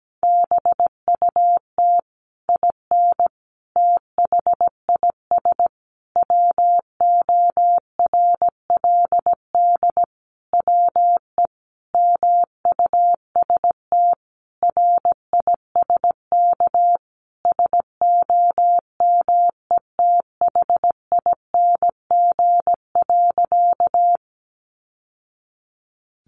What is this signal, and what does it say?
Morse code, heard as a keyed tone, about 17 wpm, 700 Hz. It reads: BUT IN THIS WORLD WE MUST RISK SOMETHING.
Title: Sentences from Top 600 Words 17wpm